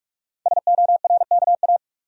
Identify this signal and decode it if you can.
Morse code, heard as a keyed tone, about 44 wpm, 690 Hz. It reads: SORKA
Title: Code Groups - 5 Letters - 45wpm